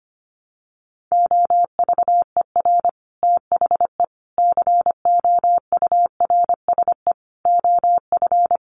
Morse code, heard as a keyed tone, about 25 wpm, 700 Hz. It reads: OVER THE COURSE OF
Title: Sets of 4 Words 25wpm